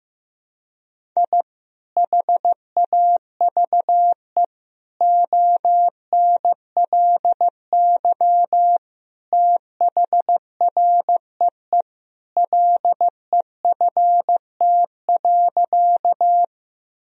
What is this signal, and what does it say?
Morse code, heard as a keyed tone, about 16 wpm, 700 Hz. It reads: I HAVE ONLY THREE LEFT.
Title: Sentences from Top 1000 Words - Review Part 1 - 15wpm